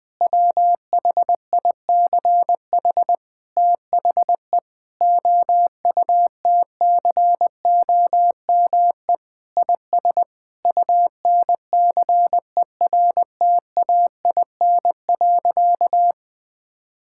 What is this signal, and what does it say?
Morse code, heard as a keyed tone, about 20 wpm, 700 Hz. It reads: WHICH THE OUTCOME IS UNCERTAIN.
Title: Sentences with 3-Word Phrases - Spoken Beginning - 20wpm